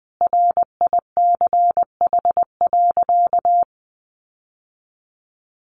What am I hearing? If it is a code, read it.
Morse code, heard as a keyed tone, about 20 wpm, 700 Hz. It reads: RICH.
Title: Sentences from Top 1100 Words 20wpm